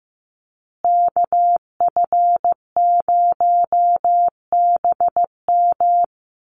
Morse code, heard as a keyed tone, about 15 wpm, 700 Hz. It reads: KF0BM
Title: USA Call Signs Repeated 2x 15wpm